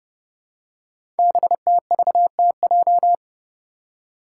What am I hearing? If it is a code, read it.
Morse code, heard as a keyed tone, about 30 wpm, 700 Hz. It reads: BTVTJ
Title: Code Groups - 5 Letters - 30wpm